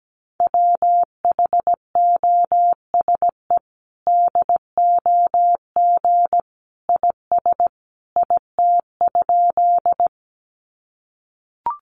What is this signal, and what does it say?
Morse code, heard as a keyed tone, about 17 wpm, 700 Hz. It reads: WHOSE DOG IS IT?
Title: Sentences from Top 1200 Words 17wpm